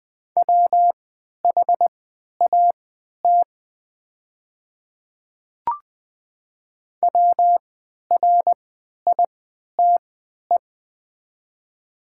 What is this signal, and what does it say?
Morse code, heard as a keyed tone, about 20 wpm, spaced as for 11 wpm, 700 Hz. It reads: WHAT WRITE
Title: 3x Character Spacing - Top 100 Words - 20wpm